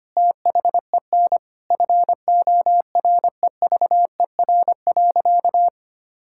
Morse code, heard as a keyed tone, about 25 wpm, 700 Hz. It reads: THEN FOREVER.
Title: Sentences from Top 1600 Words 25wpm